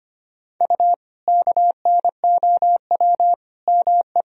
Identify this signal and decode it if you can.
Morse code, heard as a keyed tone, about 25 wpm, 700 Hz. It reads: U KNOW ME
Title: Sentences from Top 400 Words and Abbreviations 25wpm